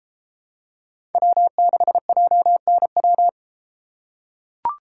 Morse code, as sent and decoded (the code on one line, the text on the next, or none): .-- -.... .--- -. .--
W6JNW